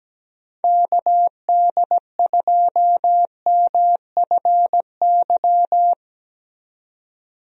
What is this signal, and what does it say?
Morse code, heard as a keyed tone, about 17 wpm, 700 Hz. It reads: KD2MFY
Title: USA Call Signs - 6 Characters - 17wpm